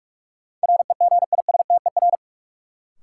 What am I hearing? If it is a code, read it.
Morse code, heard as a keyed tone, about 45 wpm, 700 Hz. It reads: REGISTER
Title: Sentences from Top 1200 Words - New Word Drill - 45wpm